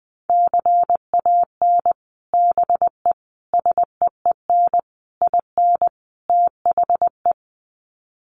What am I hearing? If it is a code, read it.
Morse code, heard as a keyed tone, about 19 wpm, 700 Hz. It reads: CAN BE SEEN IN THE
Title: Sets of 5 Words Encore2 20wpm